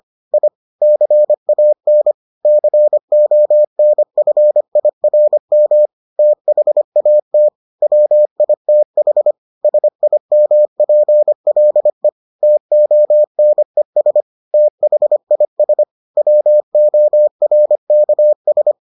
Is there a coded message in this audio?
Yes: Morse code from 0.3 to 18.7 s, heard as a keyed tone, about 25 wpm, 600 Hz. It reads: I CAN CONFIRM THAT WITH SIMPLE TONES THIS WORKS